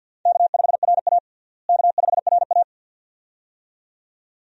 Morse code, heard as a keyed tone, about 51 wpm, 700 Hz. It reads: K5RA K5RA
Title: USA Call Signs - 4 Characters - Repeated 2x - 50wpm